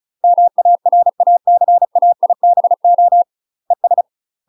Morse code, heard as a keyed tone, about 36 wpm, 700 Hz. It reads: MARACAIBO ES
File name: UK MASTER QSOs TRAINING / 35WPM / UK MASTER QSOs 0007